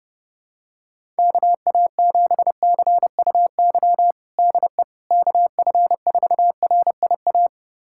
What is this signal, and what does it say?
Morse code, heard as a keyed tone, about 30 wpm, 700 Hz. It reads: KA7CUY DE KF4RIA
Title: USA Callsign DE Callsign 30wpm